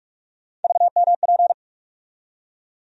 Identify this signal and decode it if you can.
Morse code, heard as a keyed tone, about 46 wpm, 700 Hz. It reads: VMP
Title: USA Call Sign Postfixes - 3 Letters 45wpm